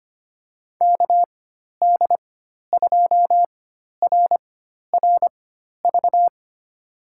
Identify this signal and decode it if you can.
Morse code, heard as a keyed tone, about 25 wpm, spaced as for 12 wpm, 700 Hz. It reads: KD2RRV
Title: USA Call Signs - 6 Characters - 4x Character Spacing - 25wpm